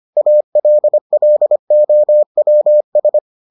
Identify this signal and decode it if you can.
Morse code, heard as a keyed tone, about 25 wpm, 600 Hz. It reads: ALLOWS